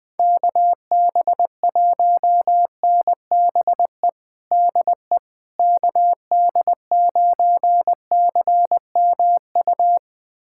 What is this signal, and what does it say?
Morse code, heard as a keyed tone, about 20 wpm, 700 Hz. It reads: KB1NBE DE KD9CMU
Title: USA Callsign DE Callsign - 6 Characters - 20wpm